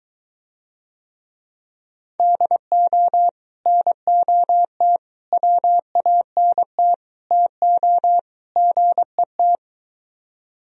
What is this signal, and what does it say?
Morse code, heard as a keyed tone, about 23 wpm, 700 Hz. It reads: DO NOT WANT TO GET